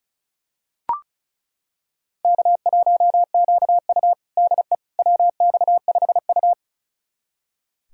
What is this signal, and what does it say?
Morse code, heard as a keyed tone, about 36 wpm, 700 Hz. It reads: K1QU DE WX5U
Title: USA Callsign DE Callsign - 4 Characters - 35wpm